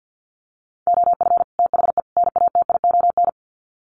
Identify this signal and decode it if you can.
Morse code, heard as a keyed tone, about 50 wpm, 700 Hz. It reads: OF THE NATION
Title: Sets of 3 Words 50wpm